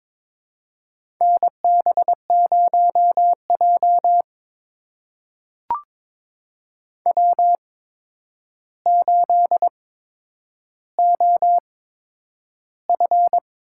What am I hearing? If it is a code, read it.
Morse code, heard as a keyed tone, about 21 wpm, spaced as for 9 wpm, 700 Hz. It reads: NB0J W8OF
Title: USA Call Signs - 4 Characters - 8x Character Spacing - 22wpm